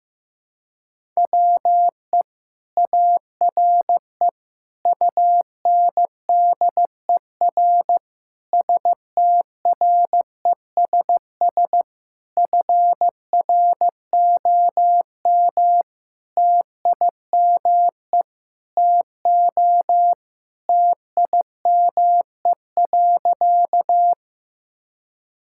Morse code, heard as a keyed tone, about 15 wpm, 700 Hz. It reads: WE ARE UNDER STRESS FROM TIME TO TIME.
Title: Sentences from Top 1000 Words - Review Part 6 - 15wpm